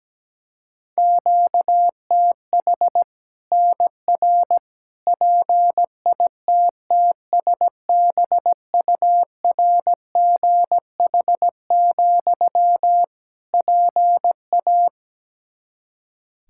Morse code, heard as a keyed tone, about 17 wpm, 700 Hz. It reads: QTH NR PITTSBURGH, PA
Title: Practice QSO - Exchange 1 - Beginning - 17wpm